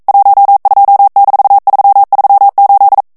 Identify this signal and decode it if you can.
Morse code, heard as a keyed tone, about 42 wpm, 800 Hz. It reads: 12-338